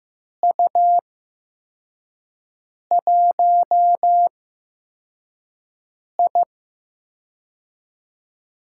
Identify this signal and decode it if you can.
Morse code, heard as a keyed tone, about 15 wpm, spaced as for 4 wpm, 700 Hz. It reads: U1I
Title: USA Call Signs - 4 Characters - 8x Character Spacing - 15wpm